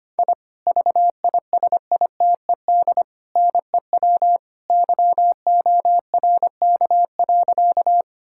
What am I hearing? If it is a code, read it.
Morse code, heard as a keyed tone, about 25 wpm, 700 Hz. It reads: I VISITED NEW YORK.
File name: practice / 25wpm / Sentences with 2-Word Phrases - 25wpm